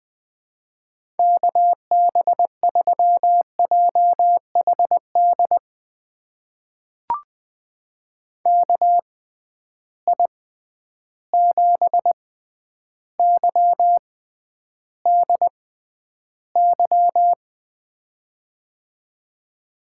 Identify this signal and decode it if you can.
Morse code, heard as a keyed tone, about 20 wpm, spaced as for 10 wpm, 700 Hz. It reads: KB3JHD KI7YDY
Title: USA Call Signs - 6 Characters - 6x Character Spacing - 20wpm